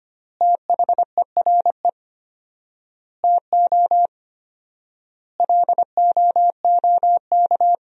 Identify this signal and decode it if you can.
Morse code, heard as a keyed tone, about 25 wpm, 700 Hz. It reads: THERE TO LOOK
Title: 4x Word-Spacing - Sentences from Top 100 Words - 25wpm